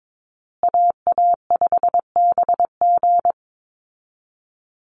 Morse code, heard as a keyed tone, about 22 wpm, 700 Hz. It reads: AA5BG
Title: Call Signs Repeated 3x - 22wpm